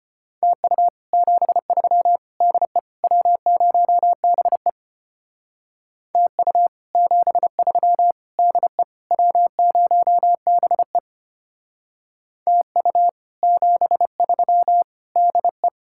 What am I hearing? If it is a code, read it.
Morse code, heard as a keyed tone, about 31 wpm, 700 Hz. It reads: TU 73 DE W0BE TU 73 DE W0BE TU 73 DE
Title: Speed-Racing Common QSO Phrases - 25wpm